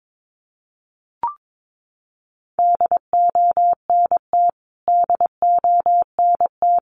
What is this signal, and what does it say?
Morse code, heard as a keyed tone, about 22 wpm, 700 Hz. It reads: DONT DONT
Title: Top 300 Words Repeated 22wpm